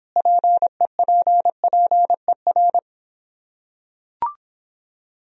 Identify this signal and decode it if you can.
Morse code, heard as a keyed tone, about 26 wpm, 700 Hz. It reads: PEPPER